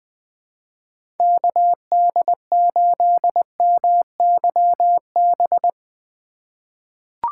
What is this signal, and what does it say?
Morse code, heard as a keyed tone, about 20 wpm, 700 Hz. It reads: KD8MYB
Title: USA Call Signs - 6 Characters - 3x Character Spacing - 20wpm